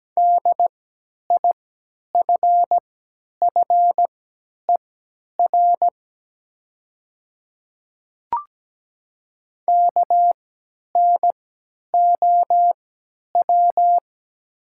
Morse code, heard as a keyed tone, about 17 wpm, spaced as for 10 wpm, 700 Hz. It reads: DIFFER KNOW
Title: 3x Character Spacing - Top 300 Words - 17wpm